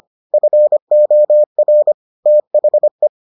Morse code, heard as a keyed tone, about 25 wpm, 600 Hz. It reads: FOR THE